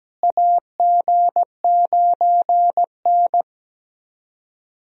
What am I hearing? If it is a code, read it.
Morse code, heard as a keyed tone, about 17 wpm, 700 Hz. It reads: AG9N
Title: USA Call Signs - 4 Characters - Repeated 3x - 17wpm